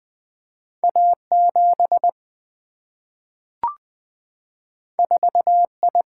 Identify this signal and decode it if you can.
Morse code, heard as a keyed tone, about 20 wpm, 700 Hz. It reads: A7 4I
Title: Single-Letter Single-Number 20wpm